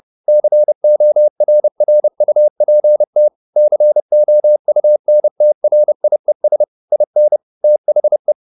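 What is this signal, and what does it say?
Morse code, heard as a keyed tone, about 30 wpm, 600 Hz. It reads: CORRUPT COUNTRIES IN THE
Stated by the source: Morse code at 30 wpm, the novel Never by Ken Follett in English, from CW med Gnister.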